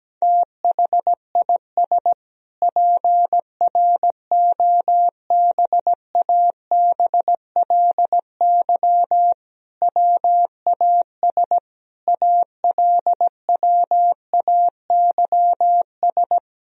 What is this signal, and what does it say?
Morse code, heard as a keyed tone, about 17 wpm, 700 Hz. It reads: THIS PROBABLY WAS ALWAYS
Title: Sentences from Top 1000 Words - Review Part 4 - 17wpm